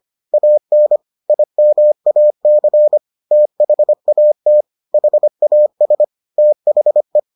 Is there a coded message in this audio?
Yes: Morse code from 0.3 to 7.2 s, heard as a keyed tone, about 25 wpm, 600 Hz. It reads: AN IMAC THAT HAS THE